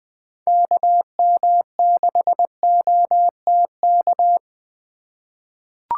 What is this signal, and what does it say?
Morse code, heard as a keyed tone, about 20 wpm, 700 Hz. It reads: KM6OTK
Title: USA Call Signs - 6 Characters - 4x Character Spacing - 20wpm